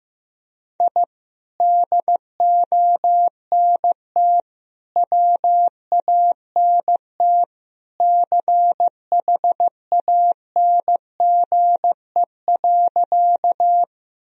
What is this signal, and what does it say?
Morse code, heard as a keyed tone, about 15 wpm, 700 Hz. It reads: I DONT WANT CHANGE.
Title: Sentences from Top 1000 Words - Review Part 1 - 15wpm